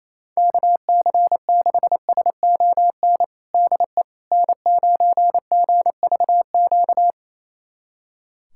Morse code, heard as a keyed tone, about 28 wpm, 700 Hz. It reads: KC6SON DE N9GVQ
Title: USA Callsign DE Callsign 28wpm